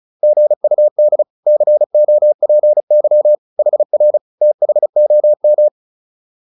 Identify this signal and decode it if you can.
Morse code, heard as a keyed tone, about 34 wpm, 600 Hz. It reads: GUD COPY HR THOM